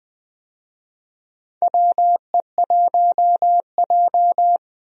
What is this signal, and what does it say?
Morse code, heard as a keyed tone, about 20 wpm, 700 Hz. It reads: WE1J
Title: USA Call Signs - 4 Characters - 20wpm